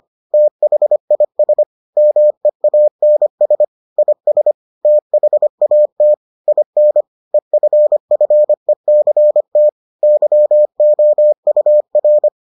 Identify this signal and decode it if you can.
Morse code, heard as a keyed tone, about 25 wpm, 600 Hz. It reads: THIS MEANS IS THAT IN EFFECT YOUR